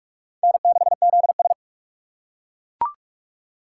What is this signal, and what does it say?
Morse code, heard as a keyed tone, about 41 wpm, 700 Hz. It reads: N6ZS
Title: USA Call Signs - 8x Character Spacing - 45wpm